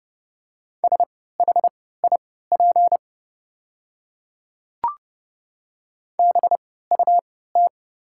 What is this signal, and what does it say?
Morse code, heard as a keyed tone, about 30 wpm, spaced as for 17 wpm, 690 Hz. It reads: SHIP BUT
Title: 3x Character Spacing - Top 500 Words - 30wpm